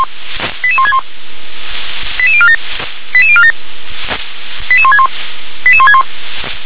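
Five-tone selective call (ZVEI), heard as quick runs of five tones, tones of about 70 ms. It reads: 88161 88047 88047 88161 88161
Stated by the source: ZVEI five-tone selective call, address 88182, received on an SDR.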